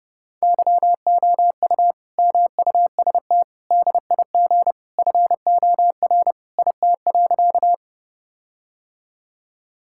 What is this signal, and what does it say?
Morse code, heard as a keyed tone, about 30 wpm, 700 Hz. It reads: YOU MUST DIG FOR IT.
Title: Sentences from Top 1600 Words 30wpm